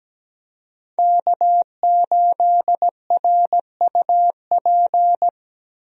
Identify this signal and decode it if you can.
Morse code, heard as a keyed tone, about 17 wpm, 700 Hz. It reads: K8RUP
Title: USA Call Signs - 5 Characters - Repeated 3x - 17wpm